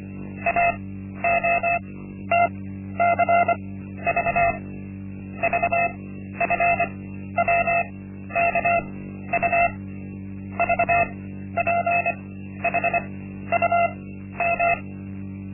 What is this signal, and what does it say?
Morse code, heard as a keyed tone, about 25 wpm, spaced as for 12 wpm, 690 Hz. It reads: AOTCV VFWKU VPHUM
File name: Linksmakalnis-1